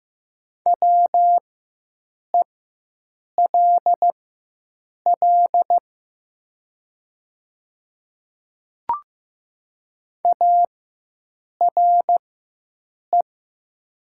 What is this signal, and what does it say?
Morse code, heard as a keyed tone, about 15 wpm, spaced as for 7 wpm, 700 Hz. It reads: WELL ARE
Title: 4x Character Spacing - Top 300 Words - 15wpm